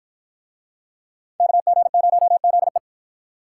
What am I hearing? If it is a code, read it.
Morse code, heard as a keyed tone, about 52 wpm, 700 Hz. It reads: KG0ZE